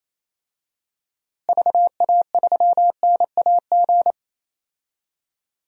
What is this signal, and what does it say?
Morse code, heard as a keyed tone, about 28 wpm, 700 Hz. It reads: VA3NAG
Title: DXCC-100 Call Signs - 6 Characters - Repeated 3x - 28wpm